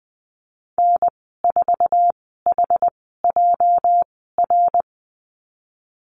Morse code, heard as a keyed tone, about 20 wpm, spaced as for 14 wpm, 700 Hz. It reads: N4HJR
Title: 245 Lesson 39g - Letter z - Call Signs 20wpm